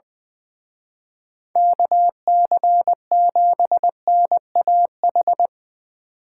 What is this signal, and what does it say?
Morse code, heard as a keyed tone, about 22 wpm, 700 Hz. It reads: KC7NAH
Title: Speed Racing - USA 2x3 Call Signs 20wpm